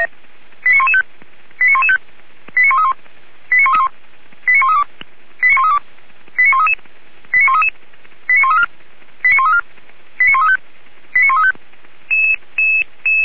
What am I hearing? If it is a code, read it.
Five-tone selective call (ZVEI), heard as quick runs of five tones, tones of about 70 ms. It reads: 89115 89115 89131 89131 89132 89132 89133 89133 89134 89134 89135 89135